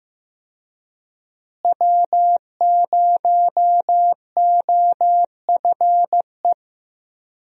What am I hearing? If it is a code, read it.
Morse code, heard as a keyed tone, about 15 wpm, 700 Hz. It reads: W0OFE
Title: USA Call Signs - 5 Characters - Repeated 3x - 15wpm